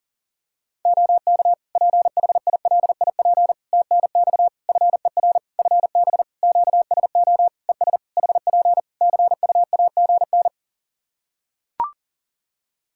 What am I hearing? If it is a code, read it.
Morse code, heard as a keyed tone, about 40 wpm, 700 Hz. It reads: OK PHILIP TNX FER FB QSO ES HP CUAGN
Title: Practice QSO - Exchange 4 - Ending - 40wpm